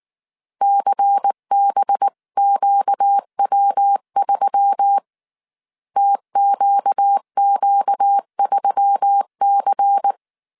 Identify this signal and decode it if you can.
Morse code, heard as a keyed tone, about 19 wpm, 800 Hz. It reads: CBQW3 TQQ3C
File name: CW / DET / 1 6 1500C S C 18wpm 2356